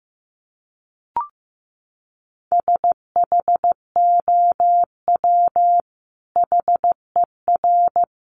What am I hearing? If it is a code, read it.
Morse code, heard as a keyed tone, about 15 wpm, 700 Hz. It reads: SHOW HER
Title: Sentences from Top 1100 Words 15wpm